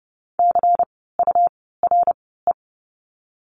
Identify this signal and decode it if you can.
Morse code, heard as a keyed tone, about 30 wpm, spaced as for 17 wpm, 700 Hz. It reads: CURE
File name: Lesson 17f - Letter u - Words 30wpm